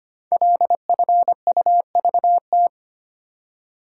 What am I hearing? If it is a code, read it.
Morse code, heard as a keyed tone, about 25 wpm, 700 Hz. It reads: LFUVT
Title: Code Groups - 5 Letters - 25wpm